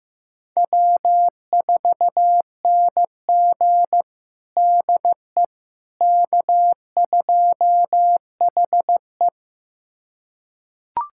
Morse code, heard as a keyed tone, about 16 wpm, 700 Hz. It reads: W4NG DE K2HE
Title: USA Callsign DE Callsign - 4 Characters - 15wpm